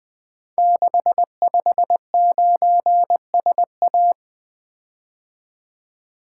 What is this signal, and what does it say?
Morse code, heard as a keyed tone, about 20 wpm, 700 Hz. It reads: 659SA
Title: Code Groups - 5 Letter:Numbers - 20wpm